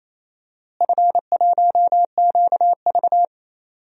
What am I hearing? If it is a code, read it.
Morse code, heard as a keyed tone, about 28 wpm, 700 Hz. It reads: F1QV
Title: DXCC-100 Call Signs - 4 Characters - Repeated 2x - 28wpm